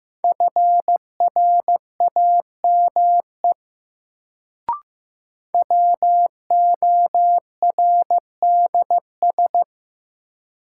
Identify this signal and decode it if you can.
Morse code, heard as a keyed tone, about 15 wpm, 700 Hz. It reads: FRAME WORDS
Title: Sentences from Top 1000 Words - Review Part 6 - New Word Drill - 15wpm